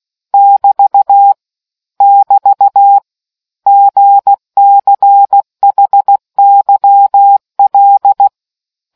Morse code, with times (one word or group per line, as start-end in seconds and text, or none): 0.3-1.3 <BT>
2.0-3.0 <BT>
3.7-8.3 GCHYL